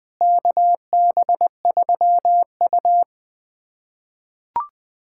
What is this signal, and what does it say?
Morse code, heard as a keyed tone, about 21 wpm, 700 Hz. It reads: KB3U
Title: USA Callsign DE Callsign - 4 Characters - 20wpm